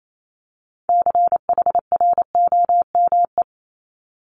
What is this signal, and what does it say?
Morse code, heard as a keyed tone, about 28 wpm, 700 Hz. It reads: CHROME
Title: Multi-letter Phonemes Repeated 3x 28wpm